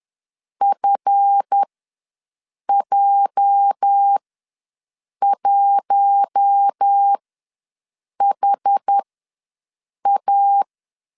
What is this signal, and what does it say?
Morse code, heard as a keyed tone, about 10 wpm, spaced as for 6 wpm, 800 Hz. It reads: FJ1HA
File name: LEZIONI / 8 YJ45 / 8 YJ ws 10wpm 4043